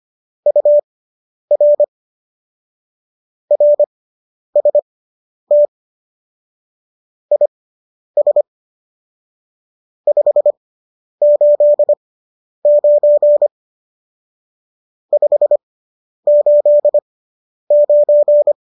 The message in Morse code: UR RST IS 589 589

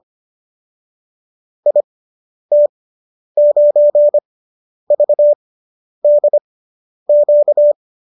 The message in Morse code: IT9VDQ

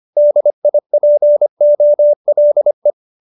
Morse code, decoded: DIPOLE

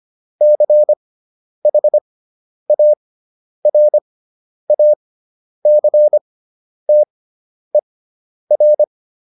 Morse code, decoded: CHARACTER

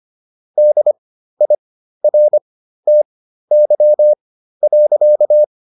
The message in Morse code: DIRTY.

